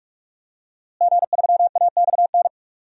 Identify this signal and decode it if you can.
Morse code, heard as a keyed tone, about 41 wpm, 700 Hz. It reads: G3AXN
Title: DXCC-100 Call Signs - 5 Characters - Repeated 3x - 45wpm